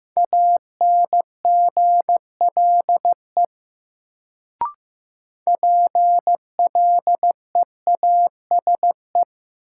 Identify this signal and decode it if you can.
Morse code, heard as a keyed tone, about 15 wpm, 700 Hz. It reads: ANGLE PLEASE
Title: Sentences from Top 1500 Words - New Word Drill Alt - 15wpm